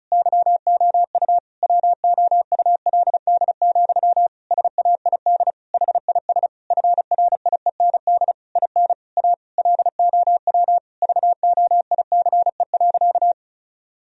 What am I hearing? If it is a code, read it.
Morse code, heard as a keyed tone, about 36 wpm, 700 Hz. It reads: YOU WOULD, SAID HIS FRIEND IN A LOW VOICE.
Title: Sentences from Top 1000 Words - Review Part 1 - 35wpm